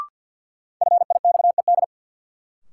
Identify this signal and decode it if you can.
Morse code, heard as a keyed tone, about 52 wpm, 700 Hz. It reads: FIXED
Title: Sentences from Top 1100 Words - New Word Drill Alt - 50wpm